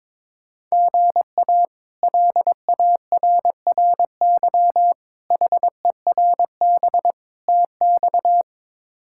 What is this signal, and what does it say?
Morse code, heard as a keyed tone, about 23 wpm, 700 Hz. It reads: GA LARRY HERB TX
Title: K1USN SST QSO Practice - 22wpm